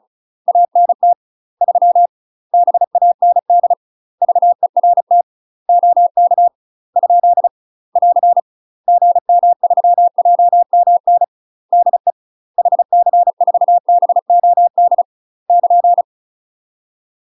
Morse code, heard as a keyed tone, about 34 wpm, 700 Hz. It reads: ANT 3 BAND VERT OK ? <AR> GM3JMN DE HC4BOD <KN>
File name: UK MASTER QSOs TRAINING / 35WPM / UK MASTER QSOs 0030